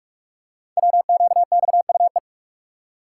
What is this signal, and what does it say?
Morse code, heard as a keyed tone, about 44 wpm, 700 Hz. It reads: WQXUE
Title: Code Groups - 5 Letters - 45wpm